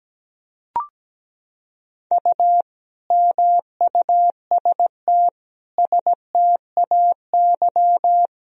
Morse code, heard as a keyed tone, about 17 wpm, 700 Hz. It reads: U MUST STAY
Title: Sentences from Top 500 Words and Abbreviations 17wpm